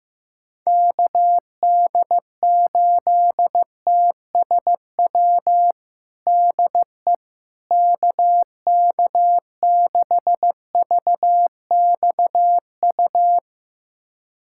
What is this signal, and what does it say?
Morse code, heard as a keyed tone, about 16 wpm, 700 Hz. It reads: KD8TSW DE KK6VXU
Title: USA Callsign DE Callsign - 6 Characters - 15wpm